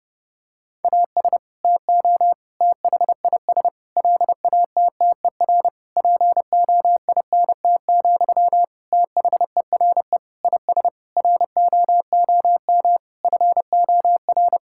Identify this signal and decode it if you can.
Morse code, heard as a keyed tone, about 32 wpm, 700 Hz. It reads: AS TO THIS LATTER POINT, THERE IS ROOM FOR
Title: Sentences from Top 1600 Words 30wpm